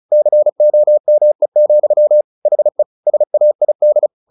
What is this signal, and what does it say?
Morse code, heard as a keyed tone, about 34 wpm, 600 Hz. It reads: COME, HE SAID
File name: the old man and the sea / the old man and the sea 1 35wpm